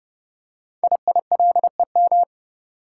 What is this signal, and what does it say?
Morse code, heard as a keyed tone, about 30 wpm, 700 Hz. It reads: IILEM
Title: Code Groups - 5 Letter:Numbers - 30wpm